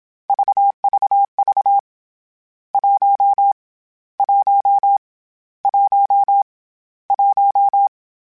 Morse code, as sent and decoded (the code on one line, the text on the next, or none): ...- ...- ...- / .---- / .---- / .---- / .----
VVV 1 1 1 1